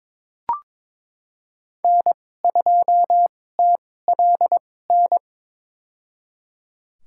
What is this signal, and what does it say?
Morse code, heard as a keyed tone, about 22 wpm, spaced as for 16 wpm, 700 Hz. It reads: N2TLN
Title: USA Call Signs - 5 Characters - 2x Character Spacing - 22wpm